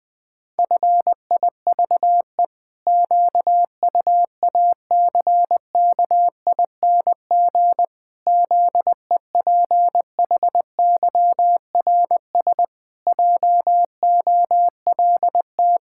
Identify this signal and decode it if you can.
Morse code, heard as a keyed tone, about 20 wpm, 700 Hz. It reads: FIVE QUACKING ZEPHYRS JOLT
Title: Pangram Challenge v2 - 20wpm